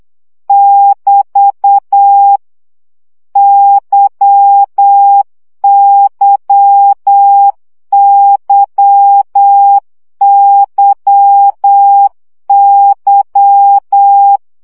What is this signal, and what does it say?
Morse code, heard as a keyed tone, about 8 wpm, 800 Hz. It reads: <BT> YYYYY
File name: CW gyakorlas y 7 baud